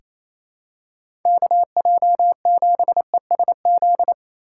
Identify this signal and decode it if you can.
Morse code, heard as a keyed tone, about 27 wpm, 700 Hz. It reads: KJ7ESZ
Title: USA Call Signs - 6 Characters - 4x Character Spacing - 28wpm